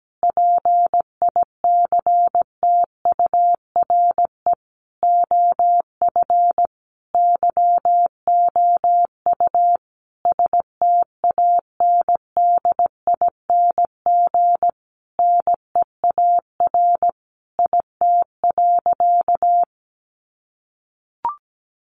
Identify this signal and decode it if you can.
Morse code, heard as a keyed tone, about 17 wpm, 700 Hz. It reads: PICTURE OF YOU STANDING NEAR IT.
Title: A Sound Of Thunder 17wpm